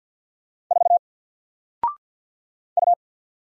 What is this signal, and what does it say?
Morse code, heard as a keyed tone, about 49 wpm, 700 Hz. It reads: 4 U
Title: Single Character (Letter:Number:Punctuation:Prosign) 50wpm